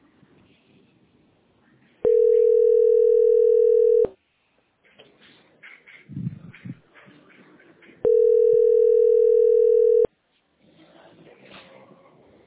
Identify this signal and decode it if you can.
Ringback tone (North America), 440 + 480 Hz.